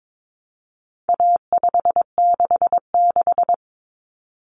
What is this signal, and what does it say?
Morse code, heard as a keyed tone, about 22 wpm, 700 Hz. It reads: A566
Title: Common Cut Numbers - 4-Digit 22wpm